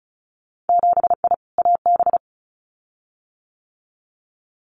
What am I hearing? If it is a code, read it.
Morse code, heard as a keyed tone, about 34 wpm, 700 Hz. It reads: 7I AB
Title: Winter Field Day Exchanges Encore 35wpm